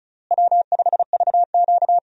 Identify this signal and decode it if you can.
Morse code, heard as a keyed tone, about 36 wpm, 700 Hz. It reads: W5VQ
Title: USA Call Signs - 4 Characters - Repeated 3x - 35wpm